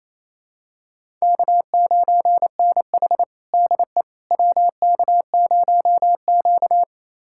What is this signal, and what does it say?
Morse code, heard as a keyed tone, about 27 wpm, 700 Hz. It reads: K9NH DE WK0Q
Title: USA Callsign DE Callsign - 4 Characters - 28wpm